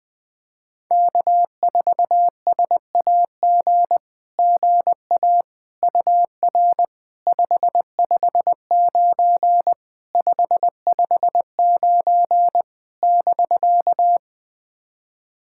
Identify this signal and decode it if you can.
Morse code, heard as a keyed tone, about 21 wpm, 700 Hz. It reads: K4SAG GA UR 559 559 <BK>
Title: SOTA QSO Practice 20wpm